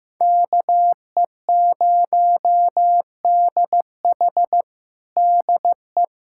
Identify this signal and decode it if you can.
Morse code, heard as a keyed tone, about 16 wpm, 700 Hz. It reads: KE0DH DE
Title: USA Callsign DE Callsign - 5 Characters - 15wpm